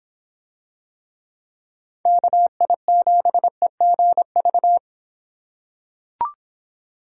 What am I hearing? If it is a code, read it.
Morse code, heard as a keyed tone, about 25 wpm, 700 Hz. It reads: KI7EGV